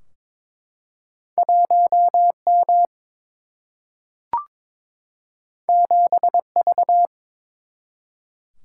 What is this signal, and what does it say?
Morse code, heard as a keyed tone, about 23 wpm, 700 Hz. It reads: 1 M 7 V
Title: Single-Letter Single-Number 22wpm